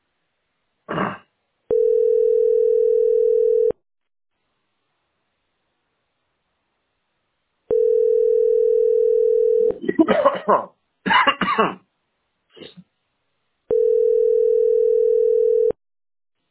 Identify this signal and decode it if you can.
Ringback tone (North America), 440 + 480 Hz.